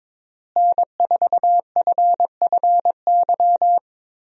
Morse code, heard as a keyed tone, about 22 wpm, 700 Hz. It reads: N4FFY
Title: USA Call Signs - 5 Characters - Repeated 3x - 22wpm